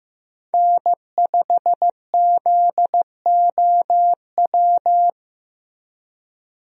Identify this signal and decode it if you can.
Morse code, heard as a keyed tone, about 16 wpm, 700 Hz. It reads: N5ZOW
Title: USA Call Signs - 5 Characters - Repeated 2x - 15wpm